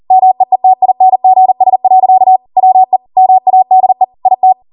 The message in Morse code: MEETINGS. WE MADE IT